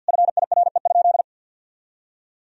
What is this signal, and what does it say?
Morse code, heard as a keyed tone, about 50 wpm, 700 Hz. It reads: FIRE?
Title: Sentences from Top 1000 Words - Review Part 3 - 50wpm